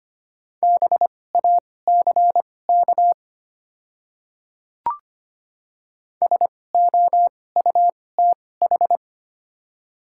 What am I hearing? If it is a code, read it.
Morse code, heard as a keyed tone, about 25 wpm, spaced as for 18 wpm, 700 Hz. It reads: BACK SOUTH
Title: 2x Character Spacing - Top 500 Words - 25wpm